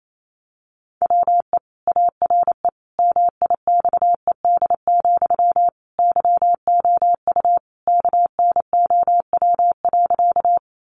Morse code, heard as a keyed tone, about 28 wpm, 700 Hz. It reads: WE ARE MIXED, YOU KNOW.
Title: Sentences from Top 1500 Words 28wpm